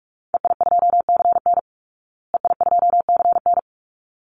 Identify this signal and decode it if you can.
Morse code, heard as a keyed tone, about 44 wpm, 700 Hz. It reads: EI2CN EI2CN
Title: Call Signs Repeated 3x - 45wpm